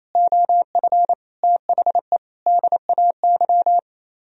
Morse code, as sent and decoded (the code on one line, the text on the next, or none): --- ..-. / - .... / -.. .- -.--
OF THE DAY